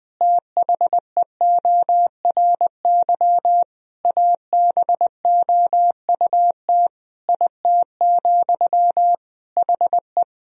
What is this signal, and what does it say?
Morse code, heard as a keyed tone, about 21 wpm, 700 Hz. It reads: THEORY ABOUT IT, HE
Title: Sentences from Top 1000 Words - Review Part 3 - 20wpm